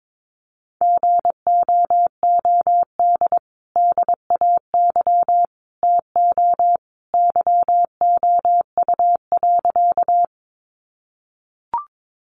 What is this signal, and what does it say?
Morse code, heard as a keyed tone, about 22 wpm, 700 Hz. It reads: GOOD DAY TO YOU.
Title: Sentences from Top 200 Words 22wpm